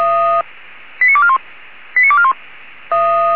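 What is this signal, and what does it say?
Five-tone selective call (ZVEI), heard as quick runs of five tones, tones of about 70 ms. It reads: 89241 89241
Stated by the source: ZVEI five-tone selective call 89378 received by SDR.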